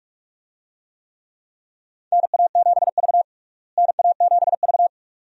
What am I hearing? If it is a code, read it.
Morse code, heard as a keyed tone, about 43 wpm, 700 Hz. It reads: NA7V NA7V
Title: USA Call Signs - 4 Characters - Repeated 2x - 45wpm